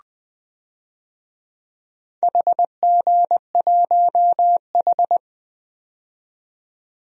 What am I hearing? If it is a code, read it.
Morse code, heard as a keyed tone, about 20 wpm, 700 Hz. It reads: HG1H
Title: DXCC-100 Call Signs - 4 Characters - 20wpm